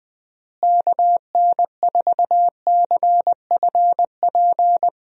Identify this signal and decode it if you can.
Morse code, heard as a keyed tone, about 20 wpm, 700 Hz. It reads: KN4CFP
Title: USA Call Signs - 6 Characters - Repeated 3x - 20wpm